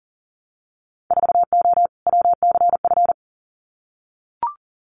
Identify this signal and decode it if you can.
Morse code, heard as a keyed tone, about 40 wpm, 700 Hz. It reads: 4O WCF
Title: Winter Field Day Exchanges Encore 40wpm